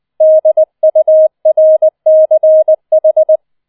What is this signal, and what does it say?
Morse code, heard as a keyed tone, about 20 wpm, 630 Hz. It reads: DURCH